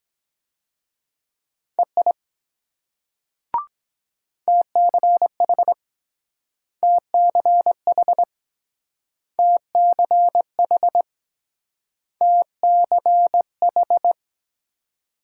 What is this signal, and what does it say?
Morse code, heard as a keyed tone, about 23 wpm, 700 Hz. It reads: EI TCH TCH TCH TCH